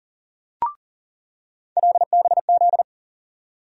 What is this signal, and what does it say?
Morse code, heard as a keyed tone, about 40 wpm, 700 Hz. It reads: LBZ
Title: USA Call Sign Postfixes - 3 Letters 40wpm